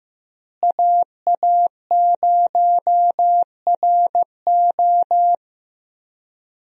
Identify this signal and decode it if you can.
Morse code, heard as a keyed tone, about 15 wpm, 700 Hz. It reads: AA0RO
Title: USA Call Signs - 5 Characters - Repeated 3x - 15wpm